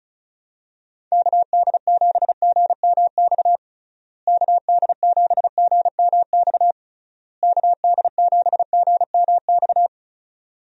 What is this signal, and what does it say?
Morse code, heard as a keyed tone, about 36 wpm, 700 Hz. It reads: KD7GMX KD7GMX KD7GMX
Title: USA Call Signs - 6 Characters - Repeated 3x - 35wpm